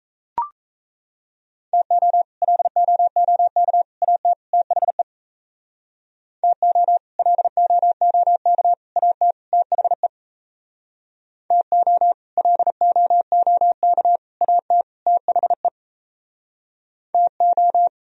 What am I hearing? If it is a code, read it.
Morse code, heard as a keyed tone, about 36 wpm, 700 Hz. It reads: TO LOOK AT THE TO LOOK AT THE TO LOOK AT THE TO